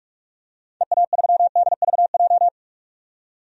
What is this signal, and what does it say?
Morse code, heard as a keyed tone, about 42 wpm, 700 Hz. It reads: EA3DVJ
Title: DXCC-100 Call Signs - 6 Characters - 45wpm